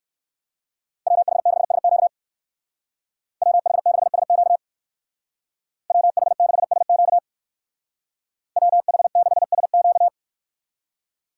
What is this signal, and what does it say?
Morse code, heard as a keyed tone, about 54 wpm, 700 Hz. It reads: WH6SQ WH6SQ WH6SQ WH6SQ
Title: Speed Racing - USA Call Signs 45wpm